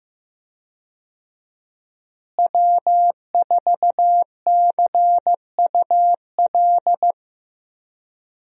Morse code, heard as a keyed tone, about 15 wpm, 700 Hz. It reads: W4CUL
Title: USA Call Signs - 5 Characters - Repeated 2x - 15wpm